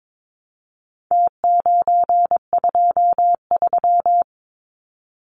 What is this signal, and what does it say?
Morse code, heard as a keyed tone, about 22 wpm, 700 Hz. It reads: T923
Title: Numbers - Contest Style - 4-Digit 22wpm